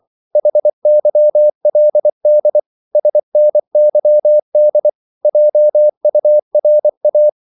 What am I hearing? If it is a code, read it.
Morse code, heard as a keyed tone, about 24 wpm, 600 Hz. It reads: HYLD SNYD JURA
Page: Korte danske ord 24wpm